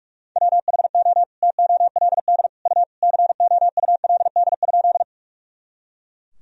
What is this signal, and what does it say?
Morse code, heard as a keyed tone, about 41 wpm, 700 Hz. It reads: WHO TOLD U COULD?
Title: Sentences from Top 400 Words and Abbreviations 45wpm